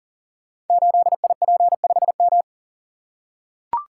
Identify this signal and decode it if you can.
Morse code, heard as a keyed tone, about 42 wpm, 700 Hz. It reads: 8IP5M
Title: Code Groups - 5 Letter:Numbers - 40wpm